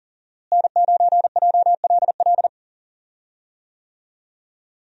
Morse code, heard as a keyed tone, about 39 wpm, 700 Hz. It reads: N9JLL
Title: USA Call Signs - 5 Characters - Repeated 3x - 40wpm